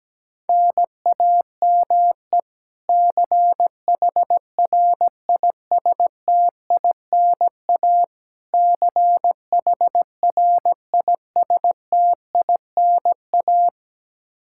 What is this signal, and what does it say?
Morse code, heard as a keyed tone, about 17 wpm, 700 Hz. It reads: NAME CHRISTINA CHRISTINA
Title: K1USN SST QSO Practice 17wpm